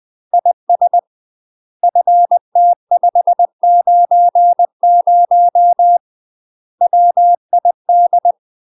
Morse code, heard as a keyed tone, about 20 wpm, 700 Hz. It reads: IS FT590 WID
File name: UK MASTER QSOs TRAINING / 20WPM / UK MASTER QSOs 0040